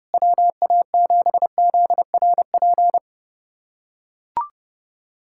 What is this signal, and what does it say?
Morse code, heard as a keyed tone, about 28 wpm, 700 Hz. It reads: WA7ZRP
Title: USA Callsign DE Callsign - 6 Characters - 30wpm